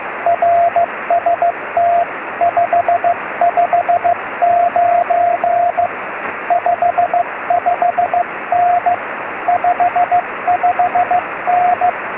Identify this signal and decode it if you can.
Morse code, heard as a keyed tone, about 15 wpm, 690 Hz, with radio noise. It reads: RST 559 55N 55N